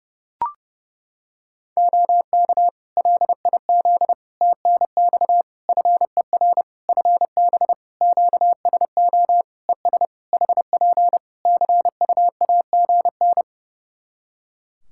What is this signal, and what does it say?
Morse code, heard as a keyed tone, about 30 wpm, 700 Hz. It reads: OK LIZ TNX FER FB QSO ES HP CUAGN
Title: Practice QSO - Exchange 4 - Ending - 30wpm